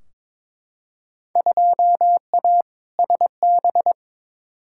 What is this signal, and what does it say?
Morse code, heard as a keyed tone, about 22 wpm, 700 Hz. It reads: 2A SB
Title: Field Day Exchanges - 22wpm